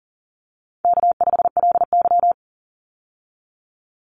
Morse code, heard as a keyed tone, about 40 wpm, 700 Hz. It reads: K5LY
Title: Easy Call Signs Repeated 2x - 40wpm